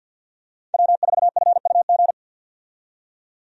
Morse code, heard as a keyed tone, about 50 wpm, 700 Hz. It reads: W4FUG
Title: USA Call Signs - 5 Characters - 50wpm